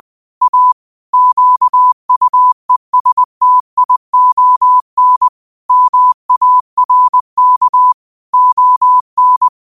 Morse code, heard as a keyed tone, about 20 wpm, 1000 Hz. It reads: A QUESTION MARK ON
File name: qotdfiles / 20wpm